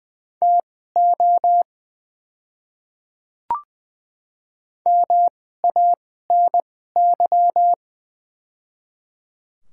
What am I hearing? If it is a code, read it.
Morse code, heard as a keyed tone, about 20 wpm, spaced as for 15 wpm, 700 Hz. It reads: TO MANY
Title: 2x Character Spacing - Top 100 Words - 20wpm